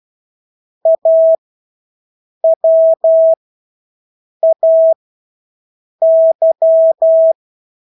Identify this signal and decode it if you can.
Morse code, heard as a keyed tone, about 12 wpm, spaced as for 6 wpm, 650 Hz. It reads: AWAY
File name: Engelske ord 12-6 wpm